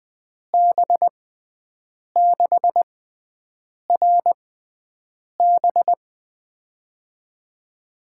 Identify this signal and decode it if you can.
Morse code, heard as a keyed tone, about 20 wpm, spaced as for 7 wpm, 700 Hz. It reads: B6RB
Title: USA Call Signs - 5 Characters - 6x Character Spacing - 20wpm